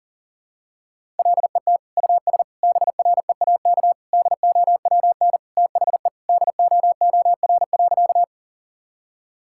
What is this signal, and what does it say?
Morse code, heard as a keyed tone, about 40 wpm, 700 Hz. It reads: LET US BREAK DOWN THE DOOR.
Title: Sentences from Top 1000 Words - Review Part 6 - 40wpm